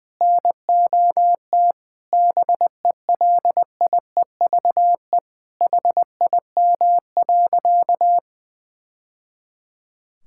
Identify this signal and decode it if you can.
Morse code, heard as a keyed tone, about 20 wpm, 700 Hz. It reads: NOT BELIEVE HIM.
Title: Sentences from Top 1000 Words - Review Part 5 - 20wpm